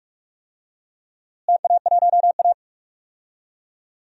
Morse code, heard as a keyed tone, about 46 wpm, 700 Hz. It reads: TA1A
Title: DXCC-100 Call Signs - 4 Characters - Repeated 3x - 45wpm